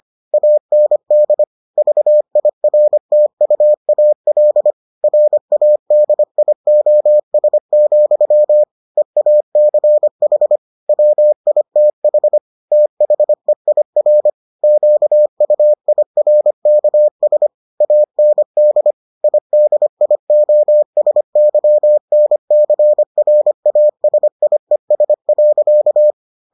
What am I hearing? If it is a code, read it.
Morse code, heard as a keyed tone, about 25 wpm, 600 Hz. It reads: AND VIRTUAL RADIOS, EACH WITH THEIR QUIRKS AND IDIOSYNCRASIES.